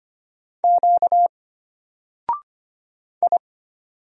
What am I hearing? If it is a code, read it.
Morse code, heard as a keyed tone, about 25 wpm, 700 Hz. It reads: Q I
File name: Single-Letter-Number-25wpm